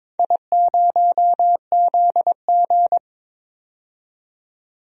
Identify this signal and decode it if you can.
Morse code, heard as a keyed tone, about 22 wpm, 700 Hz. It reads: I0ZG
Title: DXCC-100 Call Signs - 4 Characters - 22wpm